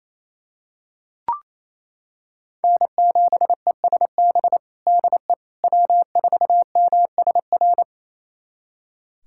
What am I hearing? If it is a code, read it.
Morse code, heard as a keyed tone, about 28 wpm, 700 Hz. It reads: N7ESB DE W4MSR
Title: USA Callsign DE Callsign - 5 Characters - 28wpm